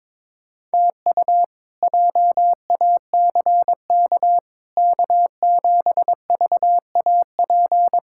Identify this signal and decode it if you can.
Morse code, heard as a keyed tone, about 22 wpm, 700 Hz. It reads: TU JACK K7VAP
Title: K1USN SST QSO Practice - 22wpm